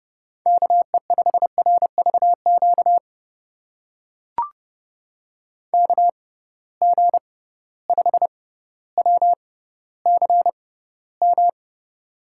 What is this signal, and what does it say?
Morse code, heard as a keyed tone, about 31 wpm, spaced as for 16 wpm, 700 Hz. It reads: KE5RVQ KG5WCM
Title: USA Call Signs - 6 Characters - 6x Character Spacing - 30wpm